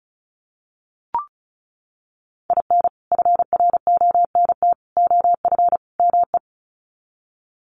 Morse code, heard as a keyed tone, about 34 wpm, 700 Hz. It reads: IN FRONT OF ME
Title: Sets of 4 Words 35wpm